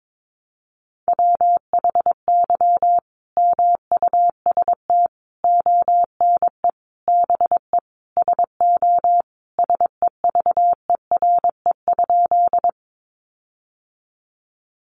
Morse code, heard as a keyed tone, about 22 wpm, 700 Hz. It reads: WHY MUST ONE BE SO SEVERE?
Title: Sentences from Top 1400 Words 22wpm